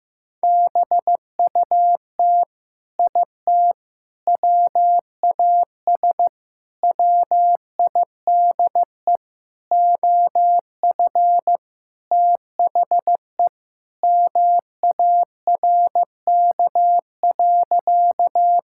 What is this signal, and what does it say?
Morse code, heard as a keyed tone, about 15 wpm, 700 Hz. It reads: BUT IT WAS WIDE OF THE MARK.
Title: Sentences from Top 1000 Words - Review Part 6 - 15wpm